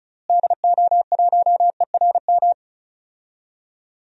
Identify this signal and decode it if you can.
Morse code, heard as a keyed tone, about 36 wpm, 700 Hz. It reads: DO1ERM
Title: DXCC-100 Call Signs - 6 Characters - Repeated 3x - 35wpm